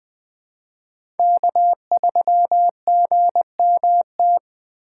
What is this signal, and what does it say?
Morse code, heard as a keyed tone, about 20 wpm, 700 Hz. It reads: K3GMT
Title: K1USN SST QSO Practice - 20wpm